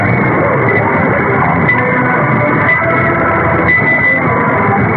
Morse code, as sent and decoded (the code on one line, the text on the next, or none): none